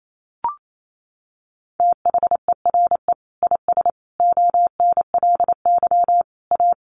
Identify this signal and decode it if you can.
Morse code, heard as a keyed tone, about 28 wpm, 700 Hz. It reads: THERE IS ONLY A
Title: Sentences from Top 1500 Words 28wpm